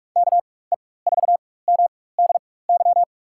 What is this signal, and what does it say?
Morse code, heard as a keyed tone, about 46 wpm, spaced as for 21 wpm, 700 Hz. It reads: KE4MDY
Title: USA Call Signs - 6 Characters - 4x Character Spacing - 45wpm